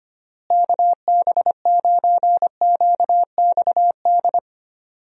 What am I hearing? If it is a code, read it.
Morse code, heard as a keyed tone, about 25 wpm, 700 Hz. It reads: KB9QXD
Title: USA Call Signs - 6 Characters - Repeated 3x - 25wpm